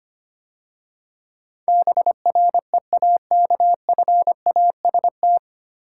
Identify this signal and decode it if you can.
Morse code, heard as a keyed tone, about 25 wpm, 700 Hz. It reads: BREAKFAST
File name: practice / 25wpm / 9 Letter Words - First 3 Letters Given - 25wpm